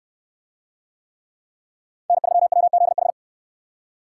Morse code, heard as a keyed tone, about 68 wpm, 700 Hz. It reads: N3WZH